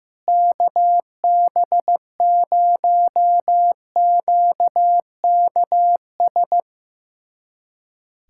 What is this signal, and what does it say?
Morse code, heard as a keyed tone, about 15 wpm, 700 Hz. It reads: KB0QKS
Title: USA Callsign DE Callsign 15wpm